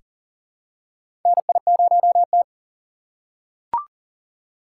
Morse code, heard as a keyed tone, about 41 wpm, 700 Hz. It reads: NI0T
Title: USA Call Signs - 4 Characters - 2x Character Spacing - 40wpm